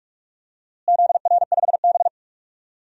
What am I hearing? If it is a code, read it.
Morse code, heard as a keyed tone, about 41 wpm, 690 Hz. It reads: ZR5B
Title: DXCC-100 Call Signs - 4 Characters - Repeated 3x - 45wpm